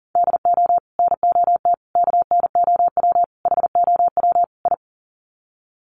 Morse code, heard as a keyed tone, about 40 wpm, 700 Hz. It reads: DO NOT KNOW HOW I
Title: Sets of 5 Words Encore2 40wpm